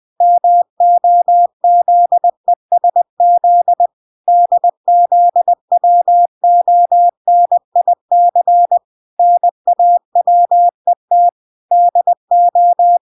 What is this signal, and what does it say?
Morse code, heard as a keyed tone, about 20 wpm, 700 Hz. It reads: MOZESZ DZWONIC NAWET DO